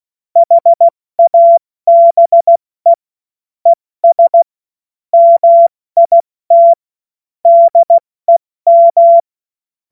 Morse code, heard as a keyed tone, about 16 wpm, spaced as for 14 wpm, 680 Hz. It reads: HABE ES MIT DEM
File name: Morsekurs / Witze / Witz 01. März 2023 70 F80